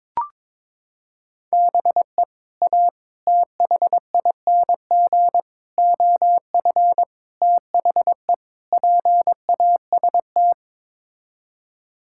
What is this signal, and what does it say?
Morse code, heard as a keyed tone, about 21 wpm, 700 Hz. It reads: BE A THING OF THE PAST
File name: practice / 22wpm / Sets of 6 Words - 22wpm